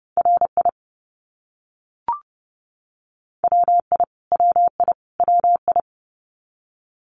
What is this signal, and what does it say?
Morse code, heard as a keyed tone, about 30 wpm, 700 Hz. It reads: RI WI WI WI
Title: US States Repeated 3x 30wpm